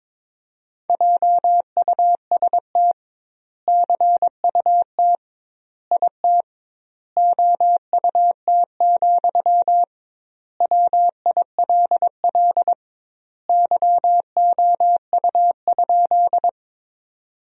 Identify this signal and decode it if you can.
Morse code, heard as a keyed tone, about 22 wpm, 700 Hz. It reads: JUST CUT IT OUT, WILL YOU?
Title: 2x Word-Spacing - Sentences from Top 300 Words - 22wpm